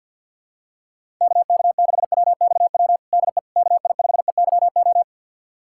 Morse code, heard as a keyed tone, about 53 wpm, 700 Hz. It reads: KK6WXW DE KI5EYO
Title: USA Callsign DE Callsign - 6 Characters - 50wpm